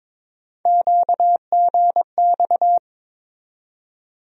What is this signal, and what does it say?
Morse code, heard as a keyed tone, about 22 wpm, 700 Hz. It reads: QGX
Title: USA Call Sign Postfixes - 3 Letters 22wpm